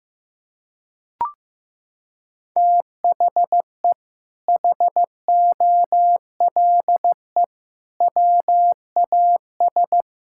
Morse code, heard as a keyed tone, about 16 wpm, 700 Hz. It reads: THE HOLE WAS
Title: Sentences from Top 1600 Words 15wpm